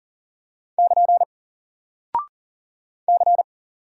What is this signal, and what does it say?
Morse code, heard as a keyed tone, about 40 wpm, 700 Hz. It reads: <KN> C
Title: Single Character (Letter:Number:Punctuation:Prosign) Alt 40wpm